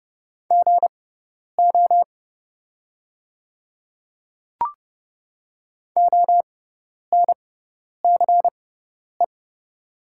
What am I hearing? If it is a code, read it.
Morse code, heard as a keyed tone, about 30 wpm, spaced as for 10 wpm, 700 Hz. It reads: GO ONCE
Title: 6x Character Spacing - Top 300 Words - 30wpm